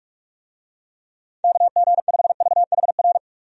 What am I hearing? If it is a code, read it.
Morse code, heard as a keyed tone, about 46 wpm, 700 Hz. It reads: KG5VHR
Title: USA Call Signs - 6 Characters - 45wpm